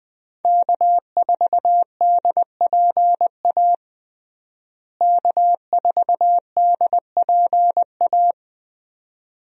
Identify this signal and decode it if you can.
Morse code, heard as a keyed tone, about 20 wpm, 700 Hz. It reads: K4DPA K4DPA
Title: USA Call Signs - 5 Characters - Repeated 2x - 20wpm